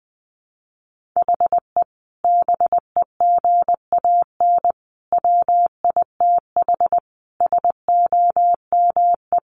Morse code, heard as a keyed tone, about 19 wpm, 700 Hz. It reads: HE BEGAN WITH SOME
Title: Sentences from Top 1500 Words 20wpm